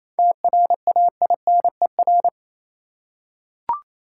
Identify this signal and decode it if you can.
Morse code, heard as a keyed tone, about 28 wpm, 700 Hz. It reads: TRAINER
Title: Seven Letter Words - 28wpm